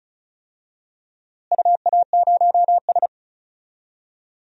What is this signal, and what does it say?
Morse code, heard as a keyed tone, about 34 wpm, 700 Hz. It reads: UA0S
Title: DXCC-100 Call Signs - 4 Characters - Repeated 3x - 35wpm